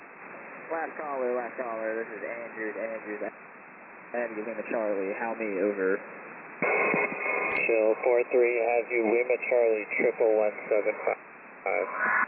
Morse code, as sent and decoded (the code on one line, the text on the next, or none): none